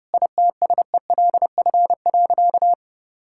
Morse code, heard as a keyed tone, about 28 wpm, 700 Hz. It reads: ITSELF.
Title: Sentences from Top 1000 Words - Review Part 5 - 30wpm